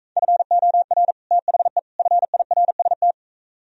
Morse code, heard as a keyed tone, about 44 wpm, 700 Hz. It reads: FOR THE FIRST